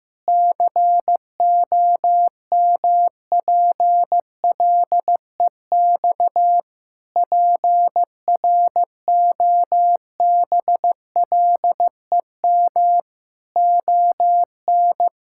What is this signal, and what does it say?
Morse code, heard as a keyed tone, about 16 wpm, 700 Hz. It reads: COMPLEX PROBLEM ON
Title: Sentences from Top 1000 Words - Review Part 6 - 15wpm